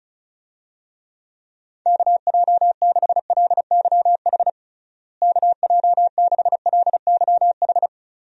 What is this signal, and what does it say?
Morse code, heard as a keyed tone, about 36 wpm, 700 Hz. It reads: KJ6LYH KJ6LYH
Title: USA Call Signs - 6 Characters - Repeated 2x - 35wpm